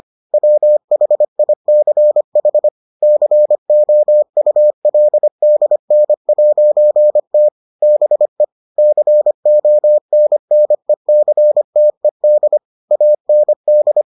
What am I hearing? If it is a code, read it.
Morse code, heard as a keyed tone, about 25 wpm, 600 Hz. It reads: WHICH COULDN'T BE CONNECTED AND